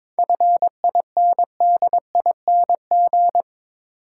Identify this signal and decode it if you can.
Morse code, heard as a keyed tone, about 22 wpm, 700 Hz. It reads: FINDING
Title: Sentences from Top 1100 Words - New Word Drill - 22wpm